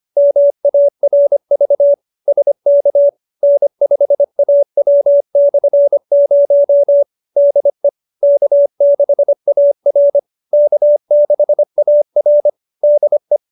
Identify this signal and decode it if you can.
Morse code, heard as a keyed tone, about 25 wpm, 570 Hz. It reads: MARV SK N5AW/0 DE K6AR K6AR DE